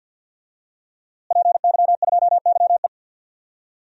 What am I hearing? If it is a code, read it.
Morse code, heard as a keyed tone, about 48 wpm, 700 Hz. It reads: PY2YE
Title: DXCC-100 Call Signs - 5 Characters - Repeated 2x - 50wpm